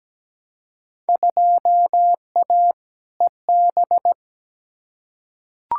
Morse code, heard as a keyed tone, about 17 wpm, 700 Hz. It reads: 2A EB
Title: Field Day Exchanges - 17wpm